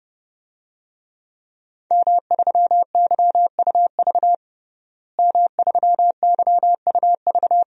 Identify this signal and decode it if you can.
Morse code, heard as a keyed tone, about 29 wpm, 700 Hz. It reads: M3YUV M3YUV
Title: DXCC-100 Call Signs - 5 Characters - Repeated 2x - 30wpm